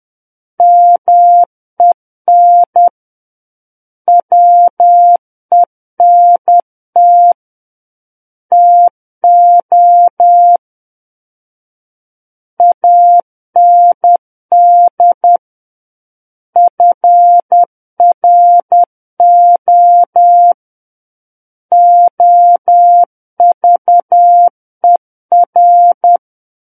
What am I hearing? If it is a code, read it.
Morse code, heard as a keyed tone, about 10 wpm, 700 Hz. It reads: MEN WENT TO AND FRO OVER
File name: Book One / 01-Chapter 1 - 10 WPM